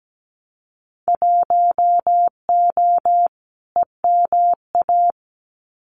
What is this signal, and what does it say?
Morse code, heard as a keyed tone, about 17 wpm, 700 Hz. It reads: 1O EMA
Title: Winter Field Day Exchanges 17wpm